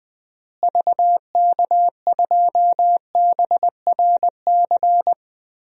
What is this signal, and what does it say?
Morse code, heard as a keyed tone, about 20 wpm, 700 Hz. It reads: VK2BRC
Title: DXCC-100 Call Signs - 6 Characters - Repeated 3x - 20wpm